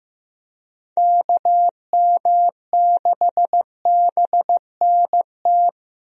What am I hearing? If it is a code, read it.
Morse code, heard as a keyed tone, about 15 wpm, 700 Hz. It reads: KM6BNT
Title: USA Call Signs - 6 Characters - Repeated 2x - 15wpm